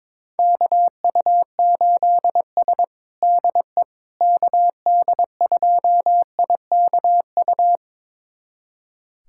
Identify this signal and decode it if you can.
Morse code, heard as a keyed tone, about 22 wpm, 700 Hz. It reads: KU8S DE KD2IKU
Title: USA Callsign DE Callsign 22wpm